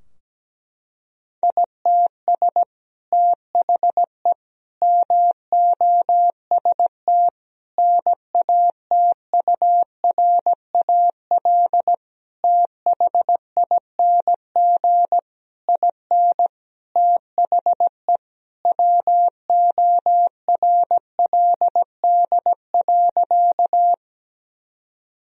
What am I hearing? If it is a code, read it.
Morse code, heard as a keyed tone, about 17 wpm, 700 Hz. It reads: ITS THE MOST NATURAL THING IN THE WORLD.
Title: Sentences from Top 1000 Words - Review Part 4 - 17wpm